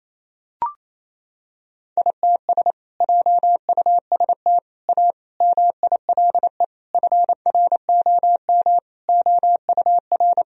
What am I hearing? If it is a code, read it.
Morse code, heard as a keyed tone, about 28 wpm, 700 Hz. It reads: ITS JUST A MILE FROM OUR
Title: Sentences from Top 1000 Words - Review Part 1 - 28wpm